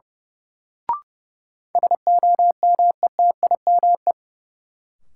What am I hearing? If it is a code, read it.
Morse code, heard as a keyed tone, about 32 wpm, 700 Hz. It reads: SOMETIME
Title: Sentences from Top 1500 Words - New Word Drill Alt - 30wpm